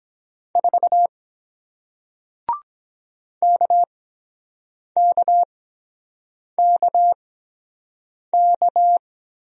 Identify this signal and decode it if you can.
Morse code, heard as a keyed tone, about 23 wpm, spaced as for 7 wpm, 700 Hz. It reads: 4 KKKK